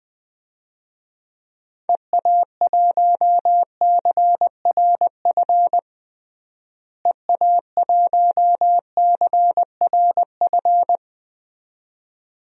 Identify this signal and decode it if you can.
Morse code, heard as a keyed tone, about 20 wpm, 700 Hz. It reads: EA1CRF EA1CRF
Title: DXCC-100 Call Signs - 6 Characters - Repeated 2x - 20wpm